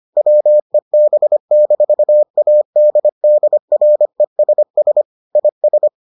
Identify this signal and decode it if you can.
Morse code, heard as a keyed tone, about 25 wpm, 600 Hz. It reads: WEB-ADDRESS IS